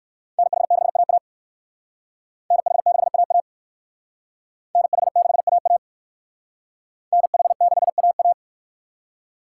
Morse code, heard as a keyed tone, about 64 wpm, 700 Hz. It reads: NH6AA NH6AA NH6AA NH6AA